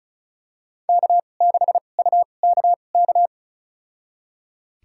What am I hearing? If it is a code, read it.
Morse code, heard as a keyed tone, about 36 wpm, spaced as for 25 wpm, 700 Hz. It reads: K6UKK
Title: USA Call Signs - 5 Characters - 2x Character Spacing - 35wpm